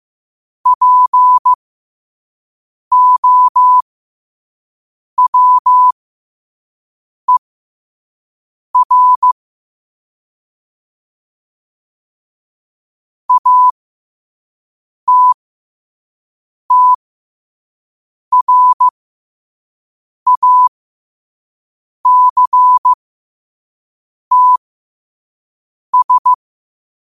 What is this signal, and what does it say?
Morse code, heard as a keyed tone, about 15 wpm, spaced as for 5 wpm, 1000 Hz. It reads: POWER ATTRACTS